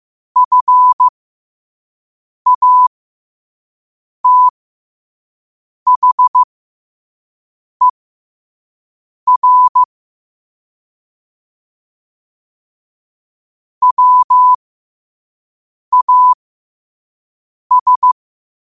Morse code, heard as a keyed tone, about 15 wpm, spaced as for 5 wpm, 1000 Hz. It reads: FATHER WAS
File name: qotdfiles / 5wpm